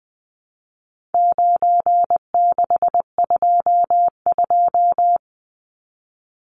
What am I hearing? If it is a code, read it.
Morse code, heard as a keyed tone, about 20 wpm, 700 Hz. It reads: 9622
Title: Numbers - 4-Digit 20wpm